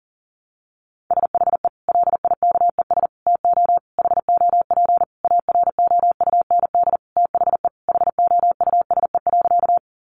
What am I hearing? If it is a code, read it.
Morse code, heard as a keyed tone, about 40 wpm, 700 Hz. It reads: SHE LIKES TO HOP AROUND THE HOUSE.
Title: Easy Phrases 40wpm